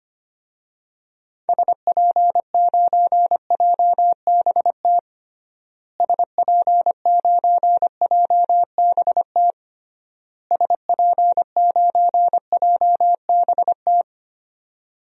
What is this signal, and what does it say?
Morse code, heard as a keyed tone, about 25 wpm, 700 Hz. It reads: SP9JBT SP9JBT SP9JBT
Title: DXCC-100 Call Signs - 6 Characters - Repeated 3x - 25wpm